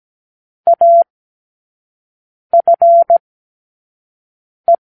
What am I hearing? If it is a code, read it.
Morse code, heard as a keyed tone, about 17 wpm, 680 Hz. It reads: A F E
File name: k7qo / Lesson9(A-F)